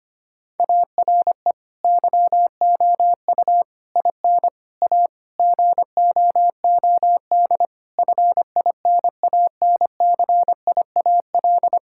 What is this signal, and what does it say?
Morse code, heard as a keyed tone, about 25 wpm, 700 Hz. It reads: ARE YOU IN A GOOD FINANCIAL
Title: Sentences from Top 1000 Words - Review Part 5 - 25wpm